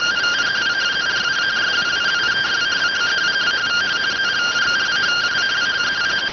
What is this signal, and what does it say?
Two-tone teleprinter signal, about 170 Hz shift.